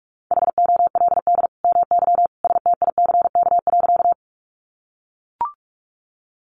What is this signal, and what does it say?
Morse code, heard as a keyed tone, about 44 wpm, 700 Hz. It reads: HOLD MY STICK.
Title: Sentences from Top 1200 Words 45wpm